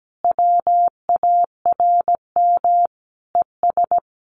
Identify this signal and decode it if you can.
Morse code, heard as a keyed tone, about 17 wpm, 700 Hz. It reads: WARM ES
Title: Common QSO Phrases 17wpm